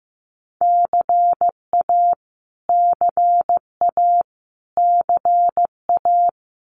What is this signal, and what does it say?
Morse code, heard as a keyed tone, about 15 wpm, 700 Hz. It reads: CA CA CA
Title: US States Repeated 3x 15wpm